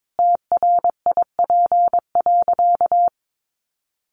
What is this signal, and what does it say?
Morse code, heard as a keyed tone, about 22 wpm, 700 Hz. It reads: TRIP.
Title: Sentences from Top 1300 Words 22wpm